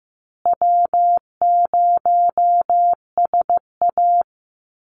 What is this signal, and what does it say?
Morse code, heard as a keyed tone, about 14 wpm, 700 Hz. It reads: W0SA
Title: Easy Call Signs Repeated 3x - 15wpm